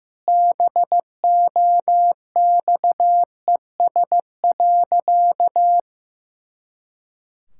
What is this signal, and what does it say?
Morse code, heard as a keyed tone, about 14 wpm, 700 Hz. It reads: BOXES.
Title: Pangram Challenge v2 - 15wpm